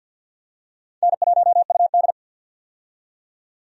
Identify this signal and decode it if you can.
Morse code, heard as a keyed tone, about 53 wpm, 700 Hz. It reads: N1UD
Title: USA Call Signs - 4 Characters - 50wpm